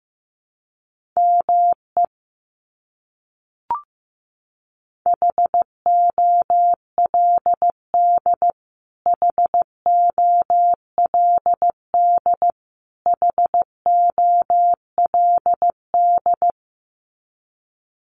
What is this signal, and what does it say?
Morse code, heard as a keyed tone, about 15 wpm, 700 Hz. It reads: ME HOLD HOLD HOLD
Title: Top 500 Words Repeated 15wpm